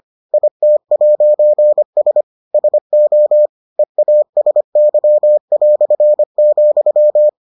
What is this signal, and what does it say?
Morse code, heard as a keyed tone, about 25 wpm, 600 Hz. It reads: IT'S SO EASY",